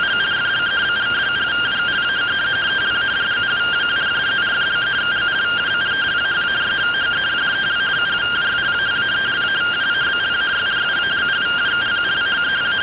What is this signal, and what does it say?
Two-tone teleprinter signal, about 170 Hz shift.